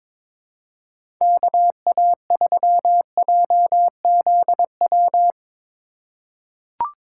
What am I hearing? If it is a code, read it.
Morse code, heard as a keyed tone, about 22 wpm, 700 Hz. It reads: KA3JZW
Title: USA Call Signs - 6 Characters - 3x Character Spacing - 22wpm